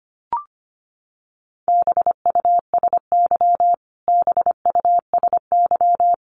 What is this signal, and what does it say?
Morse code, heard as a keyed tone, about 25 wpm, 700 Hz. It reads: BUSY BUSY
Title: Top 500 Words Repeated 25wpm